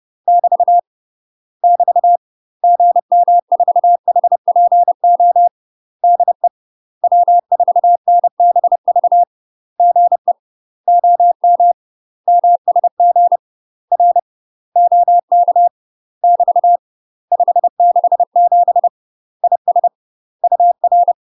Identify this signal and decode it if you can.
Morse code, heard as a keyed tone, about 30 wpm, 700 Hz. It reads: <BT> <BT> GM4HPO DE W4NBV GE OM MSG R OK <BT> 567 IS UR